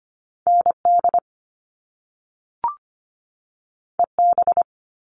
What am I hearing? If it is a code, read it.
Morse code, heard as a keyed tone, about 25 wpm, 700 Hz. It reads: N D E B
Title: All Cut Numbers - 2-Digit 25wpm